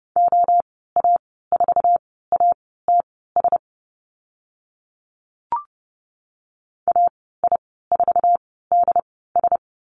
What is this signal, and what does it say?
Morse code, heard as a keyed tone, about 30 wpm, spaced as for 17 wpm, 700 Hz. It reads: OA4ATS AI4DS
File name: Lesson 14g - Number 2 - Call Signs 30wpm